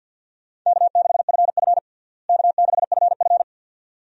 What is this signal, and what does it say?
Morse code, heard as a keyed tone, about 52 wpm, 700 Hz. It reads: K6FF K6FF
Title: USA Call Signs - 4 Characters - Repeated 2x - 50wpm